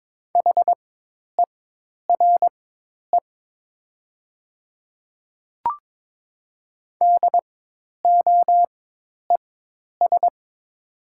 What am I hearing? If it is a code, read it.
Morse code, heard as a keyed tone, about 22 wpm, spaced as for 10 wpm, 700 Hz. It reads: HERE DOES
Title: 4x Character Spacing - Top 200 Words - 22wpm